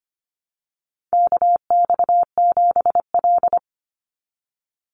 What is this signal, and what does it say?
Morse code, heard as a keyed tone, about 25 wpm, 700 Hz. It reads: KX7L
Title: NA SOTA Chaser Callsigns 25wpm